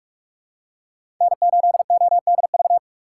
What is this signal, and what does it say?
Morse code, heard as a keyed tone, about 48 wpm, 690 Hz. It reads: N8ODV
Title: USA Call Signs - 5 Characters - 2x Character Spacing - 45wpm